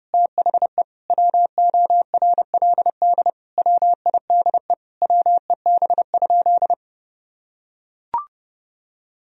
Morse code, heard as a keyed tone, about 30 wpm, 700 Hz. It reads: THE WORLD WIDE WEB?
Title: Sentences from Top 1000 Words - Review Part 4 - 30wpm